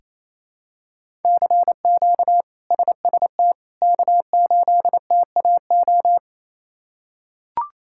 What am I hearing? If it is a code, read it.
Morse code, heard as a keyed tone, about 27 wpm, 700 Hz. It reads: CQ SST K8TAO
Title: K1USN SST QSO Practice - 28wpm